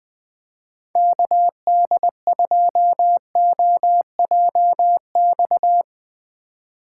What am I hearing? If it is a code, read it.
Morse code, heard as a keyed tone, about 20 wpm, 700 Hz. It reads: KD2OJX
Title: USA Call Signs - 6 Characters - 6x Character Spacing - 20wpm